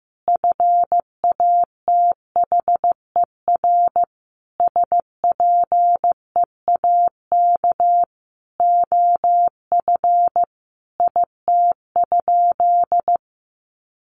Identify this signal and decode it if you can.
Morse code, heard as a keyed tone, about 15 wpm, 700 Hz. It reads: FATHER SPEAK OF IT?
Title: Sentences from Top 1100 Words 15wpm